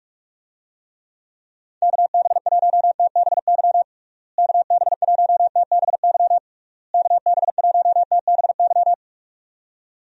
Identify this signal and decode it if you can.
Morse code, heard as a keyed tone, about 41 wpm, 700 Hz. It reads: KB1TBY KB1TBY KB1TBY
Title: USA Call Signs - 6 Characters - Repeated 3x - 45wpm